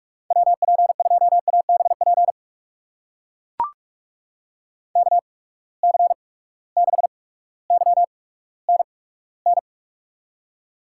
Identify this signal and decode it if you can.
Morse code, heard as a keyed tone, about 44 wpm, spaced as for 19 wpm, 700 Hz. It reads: WP2ABP KC6YNN
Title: USA Call Signs - 6 Characters - 8x Character Spacing - 45wpm